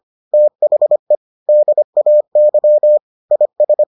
Morse code, heard as a keyed tone, about 25 wpm, 600 Hz. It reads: THE DAY IS